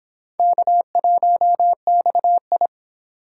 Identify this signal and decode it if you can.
Morse code, heard as a keyed tone, about 31 wpm, 700 Hz. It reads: K1XI